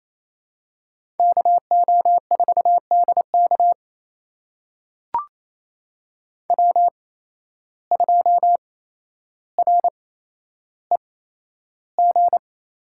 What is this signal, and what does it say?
Morse code, heard as a keyed tone, about 28 wpm, spaced as for 12 wpm, 700 Hz. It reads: KO4DK W2REG
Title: USA Call Signs - 5 Characters - 8x Character Spacing - 28wpm